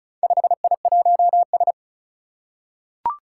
Morse code, heard as a keyed tone, about 36 wpm, 700 Hz. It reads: 5I1S
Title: Four Challenging Letter-Number Combinations 35wpm